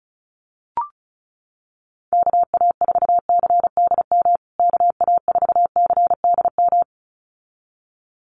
Morse code, heard as a keyed tone, about 34 wpm, 700 Hz. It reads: KA4CDM KA4CDM
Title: USA Call Signs Repeated 2x 35wpm